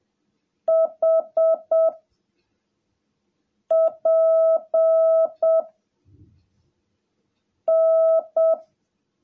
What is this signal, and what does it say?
Morse code, heard as a keyed tone, about 7 wpm, 660 Hz. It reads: H P N
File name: jh8cbh / 2021cw / cw17a